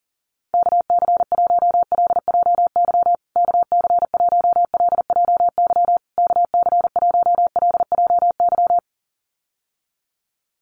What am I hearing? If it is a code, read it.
Morse code, heard as a keyed tone, about 40 wpm, 700 Hz. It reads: KC1LJY KC1LJY KC1LJY
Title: USA Call Signs Repeated 3x 40wpm